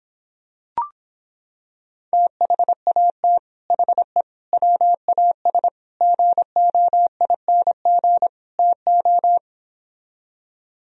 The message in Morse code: THAT HE WAS GOING TO

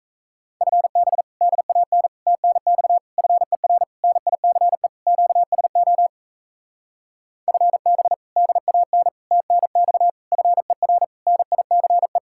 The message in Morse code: FB DAN TNX FER NICE QSO FB DAN TNX FER NICE

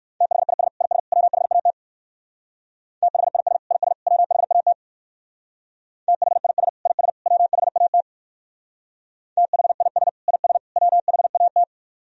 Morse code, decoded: THIS IS WHAT THIS IS WHAT THIS IS WHAT THIS IS WHAT

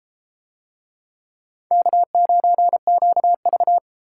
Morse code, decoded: K9QV

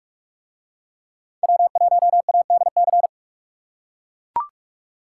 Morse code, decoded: W1ADC

